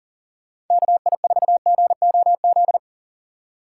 Morse code, heard as a keyed tone, about 48 wpm, 700 Hz. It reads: KI4GOZ
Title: Speed Racing - USA 2x3 Call Signs 40wpm